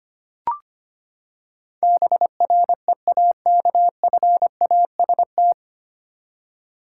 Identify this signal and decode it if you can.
Morse code, heard as a keyed tone, about 25 wpm, 700 Hz. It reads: BREAKFAST
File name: Nine Letter Words - 25wpm